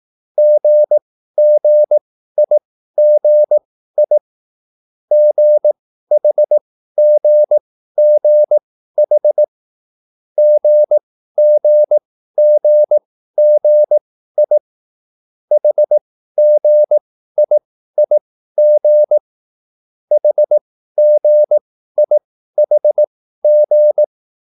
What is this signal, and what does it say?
Morse code, heard as a keyed tone, about 18 wpm, spaced as for 13 wpm, 600 Hz. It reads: GGIGI GHGGH GGGGI HGIIG HGIHG
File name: cwbook